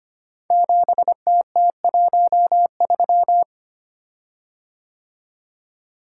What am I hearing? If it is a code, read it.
Morse code, heard as a keyed tone, about 25 wpm, 700 Hz. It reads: 7TT13
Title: Code Groups - 5 Letter:Numbers - 25wpm